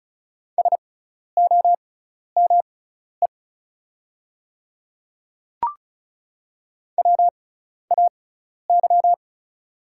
Morse code, heard as a keyed tone, about 36 wpm, spaced as for 12 wpm, 690 Hz. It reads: SOME WAY
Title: 6x Character Spacing - Top 100 Words - 35wpm